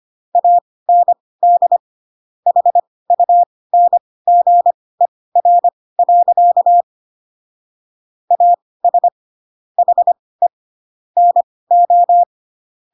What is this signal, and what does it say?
Morse code, heard as a keyed tone, about 25 wpm, spaced as for 18 wpm, 700 Hz. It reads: AND HUNGER. AS HE NO
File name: Pinocchio / Pinocchio ch6 12 25 ik1hge